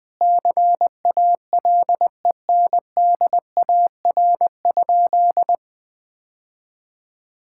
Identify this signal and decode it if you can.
Morse code, heard as a keyed tone, about 21 wpm, 700 Hz. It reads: CALENDAR?
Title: Sentences from Top 1600 Words 20wpm